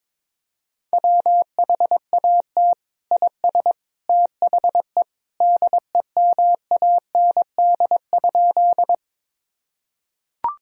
Morse code, heard as a keyed tone, about 22 wpm, 700 Hz. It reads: WHAT IS THE DEMAND?
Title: Sentences from Top 1000 Words - Review Part 4 - 22wpm